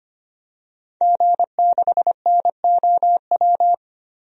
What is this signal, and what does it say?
Morse code, heard as a keyed tone, about 25 wpm, 700 Hz. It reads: G6NOW
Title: DXCC-100 Call Signs - 5 Characters - Repeated 3x - 25wpm